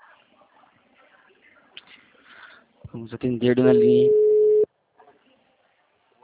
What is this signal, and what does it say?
Ringback tone (Europe, 425 Hz), with speech.